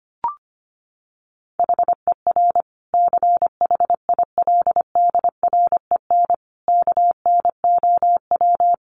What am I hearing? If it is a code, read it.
Morse code, heard as a keyed tone, about 25 wpm, 700 Hz. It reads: HER CHILDREN KNOW
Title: Sentences from Top 900 Words 25wpm